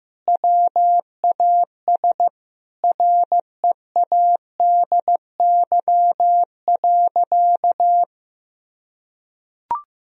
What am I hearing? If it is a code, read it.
Morse code, heard as a keyed tone, about 15 wpm, 700 Hz. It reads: WAS READY.
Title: Sentences from Top 1000 Words - Review Part 3 - 15wpm